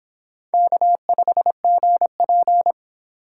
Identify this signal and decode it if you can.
Morse code, heard as a keyed tone, about 23 wpm, 700 Hz. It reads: K5GP